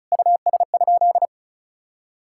Morse code, heard as a keyed tone, about 36 wpm, 700 Hz. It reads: US?
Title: Sentences from Top 1000 Words - Review Part 6 - 35wpm